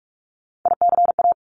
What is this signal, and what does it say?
Morse code, heard as a keyed tone, about 46 wpm, 700 Hz. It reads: ICA
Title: Top Three Letter Combinations in Words 45wpm